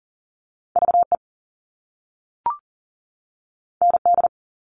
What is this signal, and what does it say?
Morse code, heard as a keyed tone, about 40 wpm, 700 Hz. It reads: VE ND
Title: All Cut Numbers - 2-Digit 40wpm